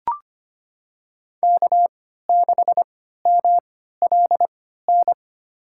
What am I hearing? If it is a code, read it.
Morse code, heard as a keyed tone, about 25 wpm, spaced as for 14 wpm, 700 Hz. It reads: K6MLN
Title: USA Call Signs - 5 Characters - 3x Character Spacing - 25wpm